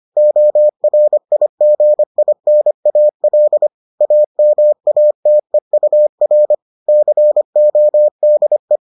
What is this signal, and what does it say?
Morse code, heard as a keyed tone, about 25 wpm, 600 Hz. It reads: ORIGINAL AMATEUR CODE